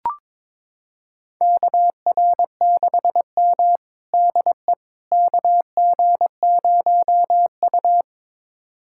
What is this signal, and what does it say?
Morse code, heard as a keyed tone, about 21 wpm, 700 Hz. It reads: KR6M DE KG0U
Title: USA Callsign DE Callsign - 4 Characters - 22wpm